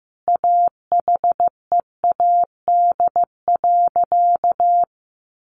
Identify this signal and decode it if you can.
Morse code, heard as a keyed tone, about 14 wpm, 700 Hz. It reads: AHEAD.
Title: Sentences from Top 1100 Words 15wpm